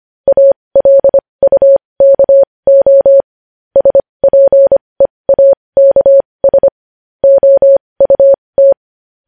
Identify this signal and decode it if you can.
Morse code, heard as a keyed tone, about 25 wpm, spaced as for 20 wpm, 550 Hz. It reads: ALUKO SPEAKS OUT
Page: news headlines in morse code at 20 wpm